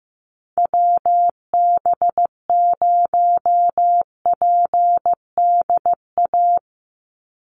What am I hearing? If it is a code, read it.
Morse code, heard as a keyed tone, about 15 wpm, 700 Hz. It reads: WB0PDA
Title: USA Call Signs 15wpm